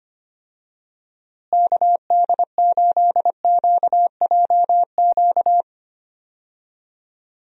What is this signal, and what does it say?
Morse code, heard as a keyed tone, about 25 wpm, 700 Hz. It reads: KD8QJQ
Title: USA Call Signs - 6 Characters - Repeated 2x - 25wpm